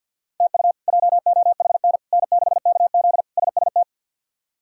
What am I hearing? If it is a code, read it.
Morse code, heard as a keyed tone, about 48 wpm, 700 Hz. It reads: TU JOHN N6KZ SST
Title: K1USN SST QSO Practice - 50wpm